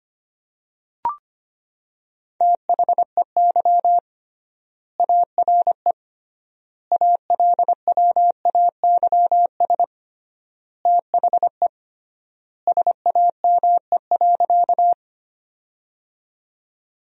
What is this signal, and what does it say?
Morse code, heard as a keyed tone, about 25 wpm, 700 Hz. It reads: THEY ARE ALWAYS THE SAME.
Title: 3x Word-Spacing - Sentences from Top 300 Words - 25wpm